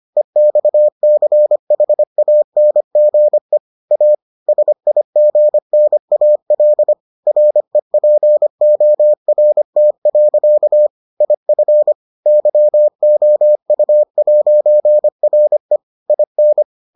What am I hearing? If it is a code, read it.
Morse code, heard as a keyed tone, about 25 wpm, 600 Hz. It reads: EXCHANGE A SIGNAL REPORT. IF YOU'RE IN